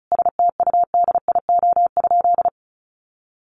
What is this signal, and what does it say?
Morse code, heard as a keyed tone, about 34 wpm, 700 Hz. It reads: STUDIO?
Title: Sentences from Top 1200 Words 35wpm